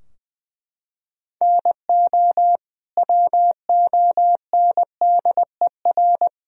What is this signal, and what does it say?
Morse code, heard as a keyed tone, about 20 wpm, 700 Hz. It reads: NO WONDER
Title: Sentences from Top 1000 Words - Review Part 3 - 20wpm